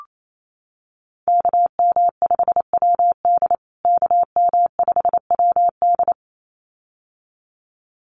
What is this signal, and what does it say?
Morse code, heard as a keyed tone, about 28 wpm, 700 Hz. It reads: KM5WD KM5WD
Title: USA Call Signs Repeated 2x 28wpm